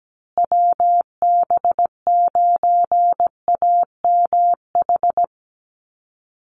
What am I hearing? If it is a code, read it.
Morse code, heard as a keyed tone, about 17 wpm, 700 Hz. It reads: WB9AMH
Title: USA Call Signs Repeated 3x 17wpm